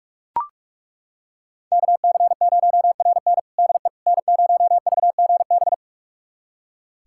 Morse code, heard as a keyed tone, about 41 wpm, 690 Hz. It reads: KC0RN DE N0VGB
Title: USA Callsign DE Callsign 45wpm